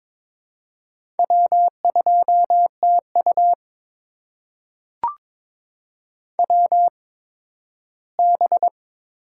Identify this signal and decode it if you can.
Morse code, heard as a keyed tone, about 21 wpm, spaced as for 13 wpm, 700 Hz. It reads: W2TU WB
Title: USA Call Signs - 8x Character Spacing - 22wpm